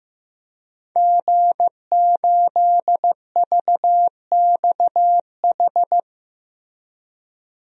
Morse code, heard as a keyed tone, about 15 wpm, 700 Hz. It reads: G8VXH
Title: DXCC-100 Call Signs - 5 Characters - Repeated 2x - 15wpm